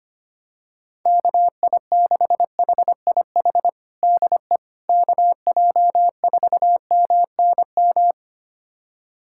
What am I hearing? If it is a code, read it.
Morse code, heard as a keyed tone, about 25 wpm, 700 Hz. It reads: KI6HIH DE KJ4MNM
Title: USA Callsign DE Callsign 25wpm